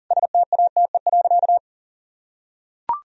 Morse code, heard as a keyed tone, about 40 wpm, 700 Hz. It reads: STATE.
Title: Sentences from Top 1000 Words - Review Part 2 - 40wpm